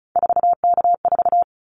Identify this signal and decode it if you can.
Morse code, heard as a keyed tone, about 36 wpm, 700 Hz. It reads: 4K4
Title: Callsign Prefixes for Top 100 DXCC - 35wpm